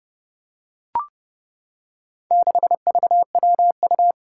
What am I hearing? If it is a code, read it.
Morse code, heard as a keyed tone, about 29 wpm, 700 Hz. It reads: 6VWU
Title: Four Challenging Letter-Number Combinations 30wpm